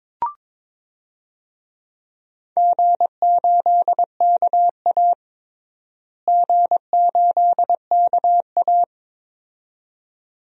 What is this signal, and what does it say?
Morse code, heard as a keyed tone, about 22 wpm, 700 Hz. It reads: G8KA G8KA
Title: DXCC-100 Call Signs - 4 Characters - Repeated 2x - 22wpm